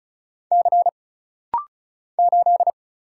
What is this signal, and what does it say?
Morse code, heard as a keyed tone, about 36 wpm, 700 Hz. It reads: C 8
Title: Single Letter-Number - Rapid-Fire 35wpm